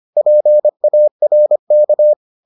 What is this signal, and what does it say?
Morse code, heard as a keyed tone, about 25 wpm, 600 Hz. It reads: PARK